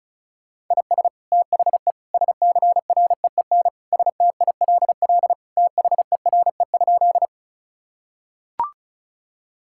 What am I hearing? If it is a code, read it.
Morse code, heard as a keyed tone, about 36 wpm, 700 Hz. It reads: IS THE SCREEN STILL THERE?
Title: Sentences from Top 1000 Words - Review Part 6 - 35wpm